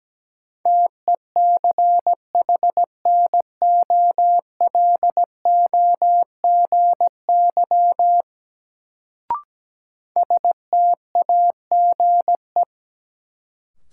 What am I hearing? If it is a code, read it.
Morse code, heard as a keyed tone, about 17 wpm, 700 Hz. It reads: TECHNOLOGY STAGE
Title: Sentences from Top 1200 Words - New Word Drill Alt - 17wpm